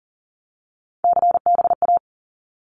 Morse code, heard as a keyed tone, about 40 wpm, 700 Hz. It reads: CBA
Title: Abbreviations 40wpm